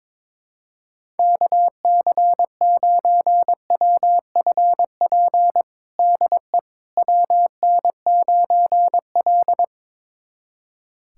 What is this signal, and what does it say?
Morse code, heard as a keyed tone, about 22 wpm, 700 Hz. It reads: KC9WFP DE WN9L
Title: USA Callsign DE Callsign 22wpm